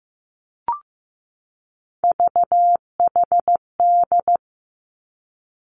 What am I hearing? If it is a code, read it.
Morse code, heard as a keyed tone, about 15 wpm, 700 Hz. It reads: VHD
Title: Three Challenging Letter-Number Combinations 15wpm